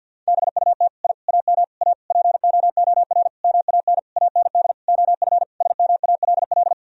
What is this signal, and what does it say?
Morse code, heard as a keyed tone, about 47 wpm, 700 Hz. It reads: BUT I AM A POOR MAN AND OF SMALL
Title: Sentences from Top 1000 Words - Review Part 3 - 50wpm